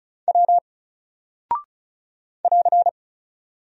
Morse code, heard as a keyed tone, about 36 wpm, 700 Hz. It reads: W <AR>
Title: Single Character (Letter:Number:Punctuation:Prosign) Alt 35wpm